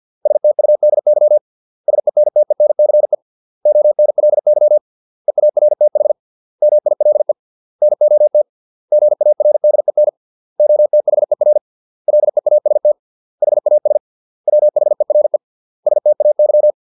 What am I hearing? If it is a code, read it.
Morse code, heard as a keyed tone, about 50 wpm, 600 Hz. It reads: STUDY SENTENCE ONLY EARTH MILE NOT GARDEN OTHER LEAST HAS WHERE STAY